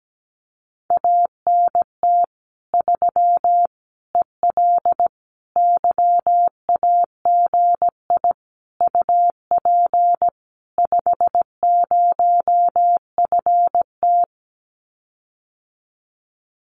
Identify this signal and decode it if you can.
Morse code, heard as a keyed tone, about 17 wpm, 700 Hz. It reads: ANT 3 EL YAGI UP 50FT
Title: Common QSO Phrases Encore 17wpm